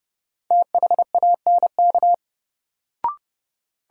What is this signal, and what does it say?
Morse code, heard as a keyed tone, about 30 wpm, 700 Hz. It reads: THANK
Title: Sets of 2 Words - New Word Drill - 30wpm